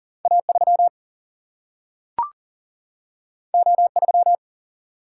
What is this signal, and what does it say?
Morse code, heard as a keyed tone, about 40 wpm, 700 Hz. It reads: A 3 O 3
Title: Single-Letter Single-Number 40wpm